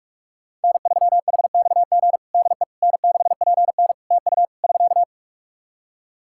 Morse code, heard as a keyed tone, about 46 wpm, 700 Hz. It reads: N3HXG DE N6PN TU <SK>
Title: K1USN SST QSO Practice 45wpm